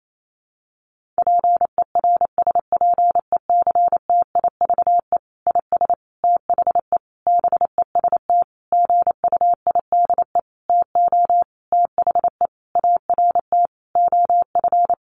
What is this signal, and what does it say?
Morse code, heard as a keyed tone, about 28 wpm, 700 Hz. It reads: PERSPECTIVE IS THE BEST GUIDE TO THE ART OF
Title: Sentences from Top 1200 Words 28wpm